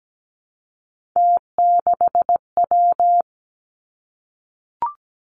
Morse code, heard as a keyed tone, about 17 wpm, 700 Hz. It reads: T6W
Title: Three Challenging Letter-Number Combinations 17wpm